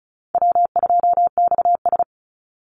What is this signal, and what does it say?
Morse code, heard as a keyed tone, about 34 wpm, 700 Hz. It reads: W2XS
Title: Easy Call Signs Repeated 2x - 35wpm